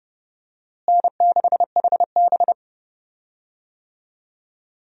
Morse code, heard as a keyed tone, about 30 wpm, 700 Hz. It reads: N6HB
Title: USA Call Signs - 4 Characters - Repeated 2x - 30wpm